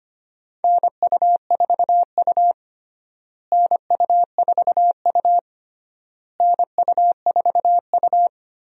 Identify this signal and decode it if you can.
Morse code, heard as a keyed tone, about 25 wpm, 700 Hz. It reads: NU4U NU4U NU4U
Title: USA Call Signs - 4 Characters - Repeated 3x - 25wpm